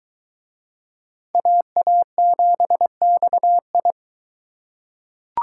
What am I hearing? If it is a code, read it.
Morse code, heard as a keyed tone, about 22 wpm, 700 Hz. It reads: AA7XI